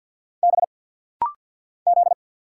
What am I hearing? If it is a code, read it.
Morse code, heard as a keyed tone, about 48 wpm, 700 Hz. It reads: B Z
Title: Single Letter-Number - Rapid-Fire 50wpm